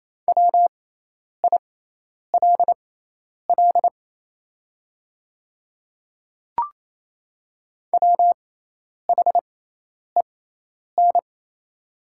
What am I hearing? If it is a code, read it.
Morse code, heard as a keyed tone, about 28 wpm, spaced as for 10 wpm, 700 Hz. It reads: WILL WHEN
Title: 6x Character Spacing - Top 100 Words - 28wpm